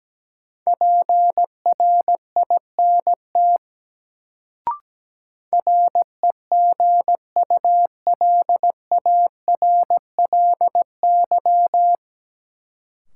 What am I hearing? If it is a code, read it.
Morse code, heard as a keyed tone, about 17 wpm, 700 Hz. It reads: PRINT REGULARLY
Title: Sentences from Top 1400 Words - New Word Drill - 17wpm